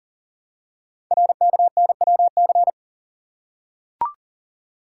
Morse code, heard as a keyed tone, about 42 wpm, 700 Hz. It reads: RKNWC
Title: Code Groups - 5 Letters - 40wpm